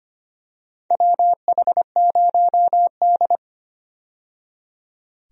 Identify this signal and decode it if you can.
Morse code, heard as a keyed tone, about 25 wpm, 700 Hz. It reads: WH0D
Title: USA Call Signs - 4 Characters - 25wpm